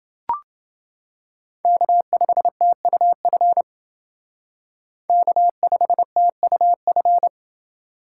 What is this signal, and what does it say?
Morse code, heard as a keyed tone, about 29 wpm, 700 Hz. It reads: K5TUF K5TUF